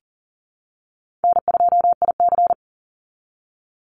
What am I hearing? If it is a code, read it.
Morse code, heard as a keyed tone, about 40 wpm, 700 Hz. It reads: N2IC
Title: Easy Call Signs Repeated 3x - 40wpm